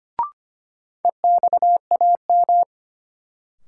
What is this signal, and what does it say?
Morse code, heard as a keyed tone, about 25 wpm, 700 Hz. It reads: EXAM
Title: Sentences from Top 1000 Words - Review Part 5 - New Word Drill - 25wpm